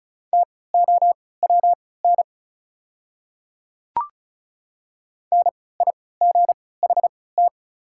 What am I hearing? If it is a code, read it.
Morse code, heard as a keyed tone, about 36 wpm, spaced as for 20 wpm, 690 Hz. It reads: TOWN NIGHT
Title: 3x Character Spacing - Top 500 Words - 35wpm